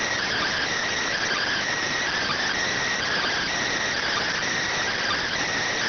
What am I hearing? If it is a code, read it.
Emergency Alert System (SAME) tones (header bursts), heard as harsh buzzing bursts. It is heard from 1.1 to 5.4 s.